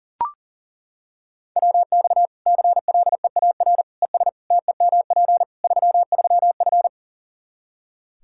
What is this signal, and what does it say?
Morse code, heard as a keyed tone, about 40 wpm, 700 Hz. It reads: WX CLEAR ES TEMP 33F
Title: Practice QSO - Exchange 2 - Info - 40wpm